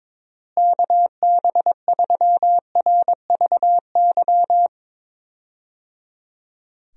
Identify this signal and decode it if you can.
Morse code, heard as a keyed tone, about 21 wpm, 700 Hz. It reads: KB3RVY
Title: USA Call Signs - 6 Characters - Repeated 3x - 22wpm